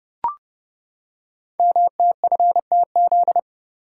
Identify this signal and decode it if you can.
Morse code, heard as a keyed tone, about 30 wpm, 700 Hz. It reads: MTFTZ
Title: Code Groups - 5 Letter:Numbers - 30wpm